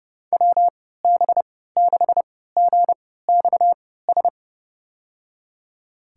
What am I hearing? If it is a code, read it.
Morse code, heard as a keyed tone, about 30 wpm, spaced as for 17 wpm, 700 Hz. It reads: WB6GXS
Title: USA Call Signs - 6 Characters - 3x Character Spacing - 30wpm